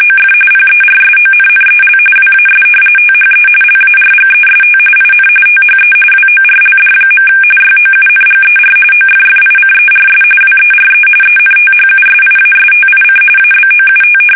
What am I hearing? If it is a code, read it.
RTTY, about 45.45 baud. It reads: OF PERSONNEL WAS THE LARGEST IN THE WORLD. YOUNGER OFFICERS WERE ENTHUSIASTIC ABOUT
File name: fsk850